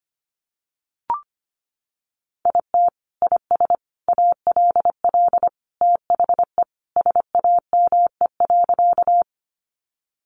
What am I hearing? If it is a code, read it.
Morse code, heard as a keyed tone, about 25 wpm, 700 Hz. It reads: IT IS ALL THE SAME.
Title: Sentences from Top 200 Words 25wpm